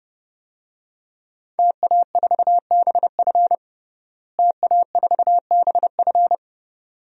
Morse code, heard as a keyed tone, about 30 wpm, 700 Hz. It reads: TA4BF TA4BF
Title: DXCC-100 Call Signs - 5 Characters - Repeated 2x - 30wpm